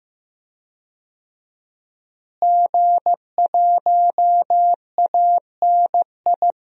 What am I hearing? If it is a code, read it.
Morse code, heard as a keyed tone, about 15 wpm, 700 Hz. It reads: G1ANI
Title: DXCC-100 Call Signs - 5 Characters - Repeated 2x - 15wpm